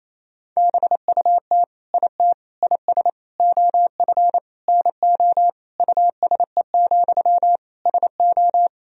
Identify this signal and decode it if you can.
Morse code, heard as a keyed tone, about 28 wpm, 700 Hz. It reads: BUT IT IS OF NO USE, SO
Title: Sentences from Top 1000 Words - Review Part 2 - 28wpm